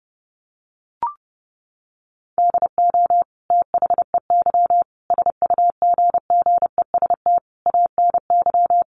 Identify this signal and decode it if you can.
Morse code, heard as a keyed tone, about 28 wpm, 700 Hz. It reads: DO THEY SUGGEST ANY
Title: Sentences from Top 1300 Words 30wpm